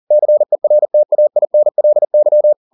Morse code, heard as a keyed tone, about 40 wpm, 600 Hz. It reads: CERTAINLY